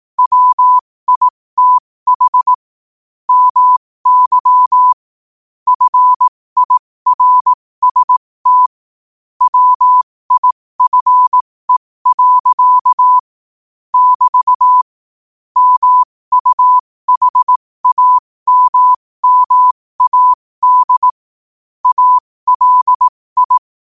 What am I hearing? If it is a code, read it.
Morse code, heard as a keyed tone, about 18 wpm, spaced as for 15 wpm, 1000 Hz. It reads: WITH MY FIRST WIFE. <BT> MUHAMMAD ALI
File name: qotdfiles / 15wpm